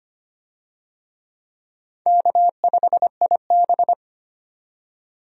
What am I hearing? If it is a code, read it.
Morse code, heard as a keyed tone, about 25 wpm, 700 Hz. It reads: K5IB
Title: USA Call Signs - 4 Characters - 25wpm